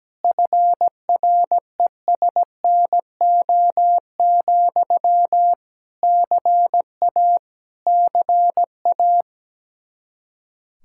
Morse code, heard as a keyed tone, about 17 wpm, 700 Hz. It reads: FRESNO, CA CA
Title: Practice QSO - Exchange 1 - Beginning - 17wpm